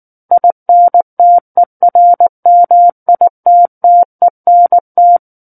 Morse code, heard as a keyed tone, about 19 wpm, 700 Hz. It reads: INTERMITTENT
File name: Book One / 10-Chapter 10 - 19 WPM